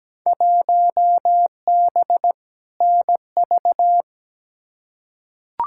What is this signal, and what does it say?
Morse code, heard as a keyed tone, about 17 wpm, 700 Hz. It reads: 1B NV
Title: Field Day Exchanges - 17wpm